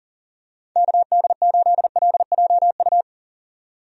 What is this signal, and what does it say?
Morse code, heard as a keyed tone, about 40 wpm, 700 Hz. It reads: KD8LJU
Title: USA Call Signs - 6 Characters - 40wpm